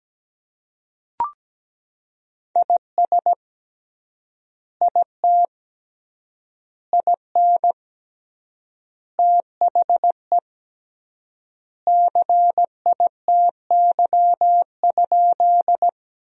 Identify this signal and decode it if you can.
Morse code, heard as a keyed tone, about 17 wpm, 700 Hz. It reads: IS IT IN THE CITY?
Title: 3x Word-Spacing - Sentences from Top 300 Words - 17wpm